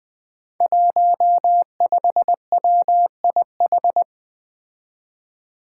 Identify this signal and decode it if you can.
Morse code, heard as a keyed tone, about 20 wpm, 700 Hz. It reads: 15WIH
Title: Five Challenging Letter-Number Combinations 20wpm